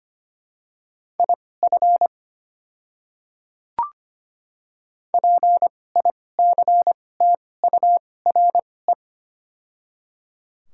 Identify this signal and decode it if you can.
Morse code, heard as a keyed tone, about 25 wpm, spaced as for 18 wpm, 700 Hz. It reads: IF PICTURE
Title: 2x Character Spacing - Top 300 Words - 25wpm